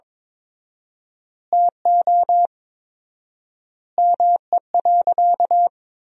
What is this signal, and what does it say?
Morse code, heard as a keyed tone, about 22 wpm, 700 Hz. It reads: TO ME.
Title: 4x Word-Spacing - Sentences from Top 200 Words - 22wpm